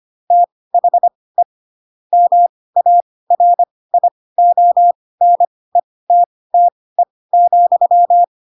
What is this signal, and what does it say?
Morse code, heard as a keyed tone, about 25 wpm, spaced as for 18 wpm, 700 Hz. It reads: THE MARIONETTE,